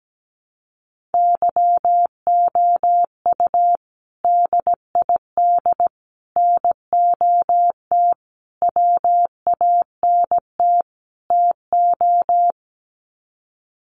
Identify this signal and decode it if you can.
Morse code, heard as a keyed tone, about 17 wpm, 700 Hz. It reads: YOU DID NOT WANT TO
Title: Sets of 5 Words Encore2 17wpm